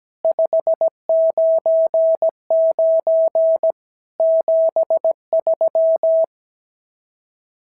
Morse code, heard as a keyed tone, about 17 wpm, 650 Hz. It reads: 599 73
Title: SOTA QSO Practice 17wpm